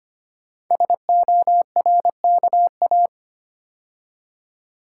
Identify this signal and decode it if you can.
Morse code, heard as a keyed tone, about 25 wpm, 700 Hz. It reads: SORKA
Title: Code Groups - 5 Letters - 25wpm